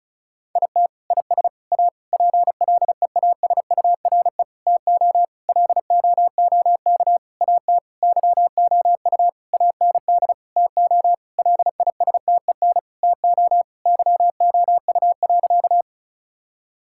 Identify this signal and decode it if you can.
Morse code, heard as a keyed tone, about 36 wpm, 700 Hz. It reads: IT IS A PLEASURE TO LOOK AT YOU AND TO LISTEN TO YOU.
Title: Sentences from Top 1600 Words 35wpm